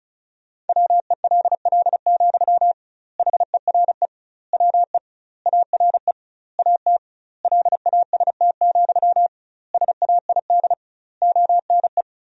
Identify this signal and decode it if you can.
Morse code, heard as a keyed tone, about 36 wpm, 700 Hz. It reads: WELL, HERE WE ARE AT LAST, SAID ONE
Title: 2x Word-Spacing - Sentences from Top 300 Words - 35wpm